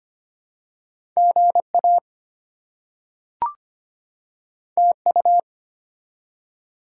Morse code, heard as a keyed tone, about 25 wpm, 700 Hz. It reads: GA TU
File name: Top 20 QSO Elements - 25wpm